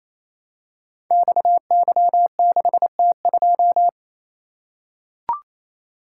Morse code, heard as a keyed tone, about 29 wpm, 700 Hz. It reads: XY6T2
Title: Code Groups - 5 Letter:Numbers - 28wpm